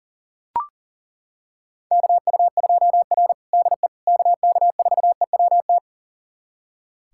Morse code, heard as a keyed tone, about 40 wpm, 700 Hz. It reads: KU2R DE KK4EWT
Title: USA Callsign DE Callsign 40wpm